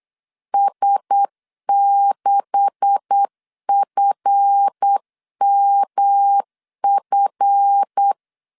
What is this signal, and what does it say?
Morse code, heard as a keyed tone, about 8 wpm, 800 Hz. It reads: S6FMF
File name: CW / DET / 1 9 1500C S C 8wpm 5653